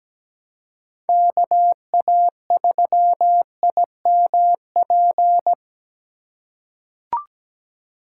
Morse code, heard as a keyed tone, about 17 wpm, 700 Hz. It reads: KA3IMP
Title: USA Call Signs - 6 Characters - 3x Character Spacing - 17wpm